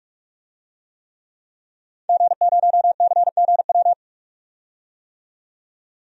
Morse code, heard as a keyed tone, about 46 wpm, 700 Hz. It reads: G0CGW
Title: DXCC-100 Call Signs - 5 Characters - 45wpm